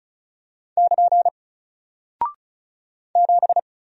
Morse code, heard as a keyed tone, about 36 wpm, 700 Hz. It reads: <KN> 7
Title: Single Character (Letter:Number:Punctuation:Prosign) Alt 35wpm